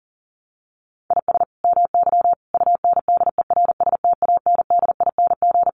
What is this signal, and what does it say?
Morse code, heard as a keyed tone, about 40 wpm, 700 Hz. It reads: IS MY UNDERSTANDING
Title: Sentences from Top 1400 Words 40wpm